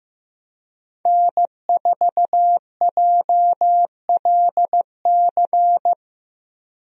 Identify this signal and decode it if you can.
Morse code, heard as a keyed tone, about 15 wpm, 700 Hz. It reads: N4JLC
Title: USA Call Signs - 5 Characters - 15wpm